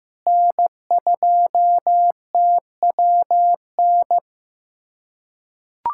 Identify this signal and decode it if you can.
Morse code, heard as a keyed tone, about 16 wpm, 700 Hz. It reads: N2TWN
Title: USA Callsign DE Callsign - 5 Characters - 15wpm